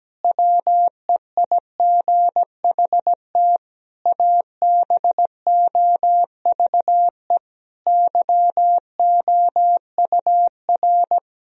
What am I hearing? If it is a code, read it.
Morse code, heard as a keyed tone, about 17 wpm, 700 Hz. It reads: WEIGHT ABOVE YOUR
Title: Sentences from Top 1000 Words - Review Part 3 - 17wpm